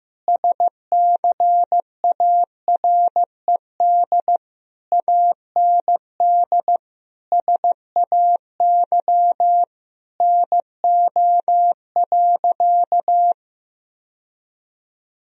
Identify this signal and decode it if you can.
Morse code, heard as a keyed tone, about 15 wpm, 700 Hz. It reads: SCARED AND SAY NO.
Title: Sentences from Top 1000 Words - Review Part 4 - 15wpm